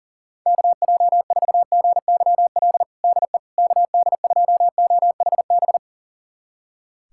Morse code, heard as a keyed tone, about 40 wpm, 700 Hz. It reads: KJ4GYL DE KD2OHB
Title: USA Callsign DE Callsign - 6 Characters - 40wpm